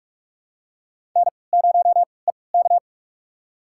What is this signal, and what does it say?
Morse code, heard as a keyed tone, about 47 wpm, spaced as for 26 wpm, 700 Hz. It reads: N0EK
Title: USA Call Signs - 4 Characters - 3x Character Spacing - 45wpm